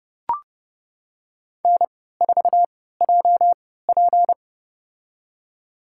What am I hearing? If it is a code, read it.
Morse code, heard as a keyed tone, about 32 wpm, spaced as for 17 wpm, 700 Hz. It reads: N4JP
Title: USA Call Signs - 4 Characters - 3x Character Spacing - 30wpm